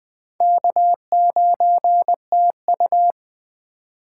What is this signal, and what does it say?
Morse code, heard as a keyed tone, about 22 wpm, 700 Hz. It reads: K9TU